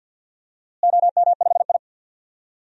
Morse code, heard as a keyed tone, about 51 wpm, 700 Hz. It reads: OM5I
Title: DXCC-100 Call Signs - 4 Characters - Repeated 3x - 50wpm